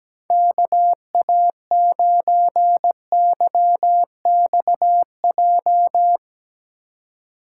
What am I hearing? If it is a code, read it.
Morse code, heard as a keyed tone, about 17 wpm, 700 Hz. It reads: KA9YXJ
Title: Speed Racing - USA 2x3 Call Signs 17wpm